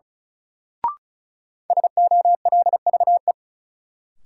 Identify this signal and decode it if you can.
Morse code, heard as a keyed tone, about 34 wpm, 700 Hz. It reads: SOLVE
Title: Sentences from Top 1000 Words - New Word Drill - 35wpm